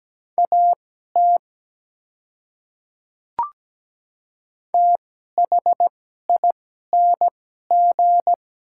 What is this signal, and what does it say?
Morse code, heard as a keyed tone, about 17 wpm, spaced as for 12 wpm, 700 Hz. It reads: AT THING
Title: 2x Character Spacing - Top 100 Words - 17wpm